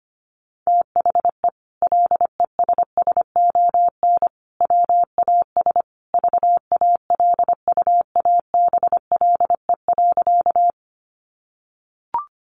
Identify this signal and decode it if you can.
Morse code, heard as a keyed tone, about 25 wpm, 700 Hz. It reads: THE LESSON WAS VALUABLE.
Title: Sentences from Top 1200 Words 25wpm